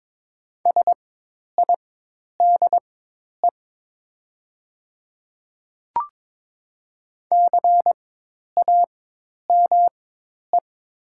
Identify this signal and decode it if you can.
Morse code, heard as a keyed tone, about 22 wpm, spaced as for 10 wpm, 700 Hz. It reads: SIDE CAME
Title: 4x Character Spacing - Top 200 Words - 22wpm